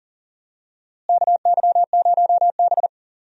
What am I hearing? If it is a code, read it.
Morse code, heard as a keyed tone, about 40 wpm, 700 Hz. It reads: KY0B
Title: USA Call Signs - 4 Characters - Repeated 3x - 40wpm